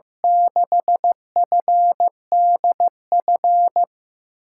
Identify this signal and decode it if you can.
Morse code, heard as a keyed tone, about 15 wpm, 700 Hz. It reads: 6FDF
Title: USA Call Signs - 6 Characters - 8x Character Spacing - 15wpm